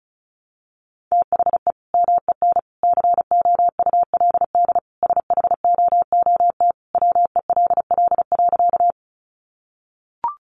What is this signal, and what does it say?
Morse code, heard as a keyed tone, about 34 wpm, 700 Hz. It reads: THE MEN COULD SHOOT WELL.
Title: Sentences from Top 1200 Words 35wpm